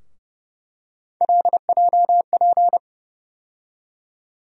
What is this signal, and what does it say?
Morse code, heard as a keyed tone, about 30 wpm, 700 Hz. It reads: LJP
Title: USA Call Sign Postfixes - 3 Letters 30wpm